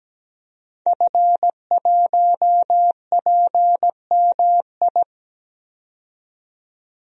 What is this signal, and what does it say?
Morse code, heard as a keyed tone, about 17 wpm, 700 Hz. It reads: F1PMI
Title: DXCC-100 Call Signs - 5 Characters - 17wpm